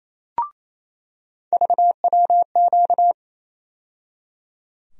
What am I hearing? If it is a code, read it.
Morse code, heard as a keyed tone, about 28 wpm, 700 Hz. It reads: VWQ
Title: USA Call Sign Postfixes - 3 Letters 28wpm